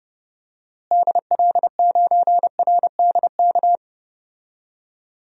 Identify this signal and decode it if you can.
Morse code, heard as a keyed tone, about 30 wpm, 700 Hz. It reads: DL9RDK
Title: DXCC-100 Call Signs - 6 Characters - Repeated 3x - 30wpm